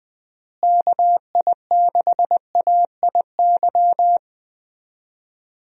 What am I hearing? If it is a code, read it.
Morse code, heard as a keyed tone, about 20 wpm, 700 Hz. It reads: KI6AIY
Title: USA Call Signs - 6 Characters - Repeated 3x - 20wpm